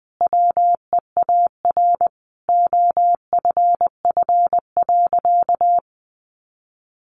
Morse code, heard as a keyed tone, about 19 wpm, 700 Hz. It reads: WEAR OFF.
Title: Sentences from Top 1500 Words 20wpm